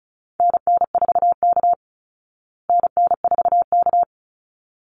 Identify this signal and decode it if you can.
Morse code, heard as a keyed tone, about 34 wpm, 700 Hz. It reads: NN4K NN4K
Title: Call Signs Repeated 3x - 35wpm